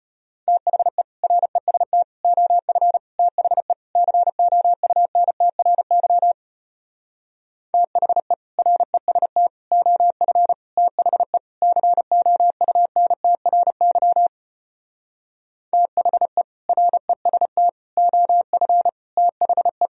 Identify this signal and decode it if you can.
Morse code, heard as a keyed tone, about 33 wpm, 700 Hz. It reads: THE REST OF THE COUNTRY THE REST OF THE COUNTRY THE REST OF THE